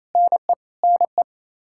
Morse code, heard as a keyed tone, about 27 wpm, 700 Hz. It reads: NE NE
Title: Practice QSO - Exchange 1 - Beginning - 28wpm